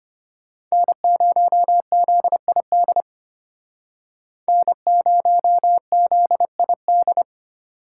Morse code, heard as a keyed tone, about 32 wpm, 700 Hz. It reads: N0ZID N0ZID